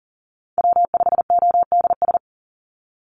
Morse code, heard as a keyed tone, about 40 wpm, 700 Hz. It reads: W5ODS
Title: NA SOTA Activator Callsigns 40wpm